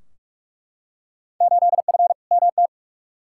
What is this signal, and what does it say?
Morse code, heard as a keyed tone, about 47 wpm, 700 Hz. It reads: 8F MT
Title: Field Day Exchanges Encore - 45wpm